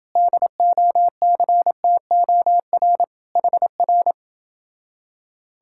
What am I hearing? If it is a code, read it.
Morse code, heard as a keyed tone, about 29 wpm, 700 Hz. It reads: DOCTOR HR